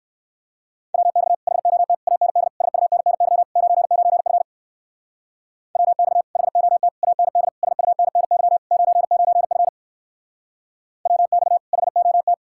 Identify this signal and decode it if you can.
Morse code, heard as a keyed tone, about 59 wpm, 700 Hz. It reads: WX HOT AND SUNNY 99F WX HOT AND SUNNY 99F WX HOT